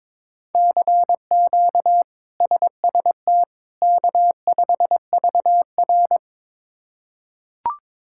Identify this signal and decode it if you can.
Morse code, heard as a keyed tone, about 22 wpm, 700 Hz. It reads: CQ SST K5VR
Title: K1USN SST QSO Practice - 22wpm